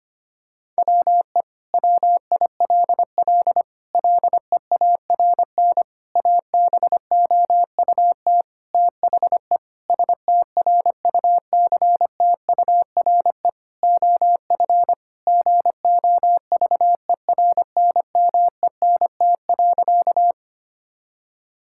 Morse code, heard as a keyed tone, about 25 wpm, 700 Hz. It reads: WE WILL LEARN ABOUT THE STRUCTURE OF GOVERNMENT.
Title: Sentences from Top 1000 Words - Review Part 6 - 25wpm